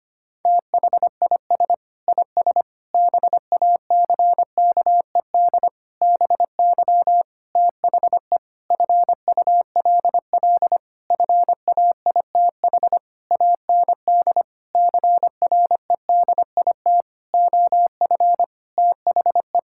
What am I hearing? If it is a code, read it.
Morse code, heard as a keyed tone, about 25 wpm, 700 Hz. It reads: THIS IS BACKED BY THE FULL FAITH AND CREDIT OF THE
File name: practice / 25wpm / Sentences with 9-Word Phrases - 25wpm